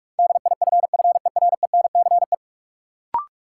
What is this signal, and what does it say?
Morse code, heard as a keyed tone, about 46 wpm, 700 Hz. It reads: DIFFERENCE
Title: Sentences from Top 800 Words - New Word Drill - 45wpm